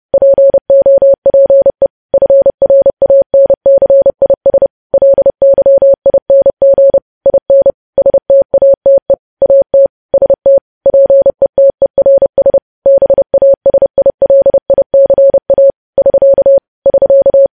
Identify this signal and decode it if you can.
Morse code, heard as a keyed tone, about 30 wpm, 550 Hz. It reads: POPE FRANCIS LYING IN STATE AT ST PETERS BASILICA <SK> <SK>
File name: podcast / News headlines-1x-30x30